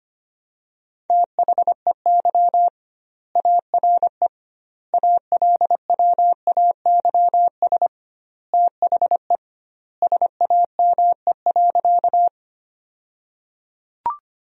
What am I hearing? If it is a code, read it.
Morse code, heard as a keyed tone, about 25 wpm, 700 Hz. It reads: THEY ARE ALWAYS THE SAME.
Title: 2x Word-Spacing - Sentences from Top 300 Words - 25wpm